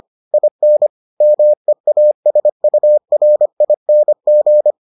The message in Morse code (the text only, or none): IN MEASURING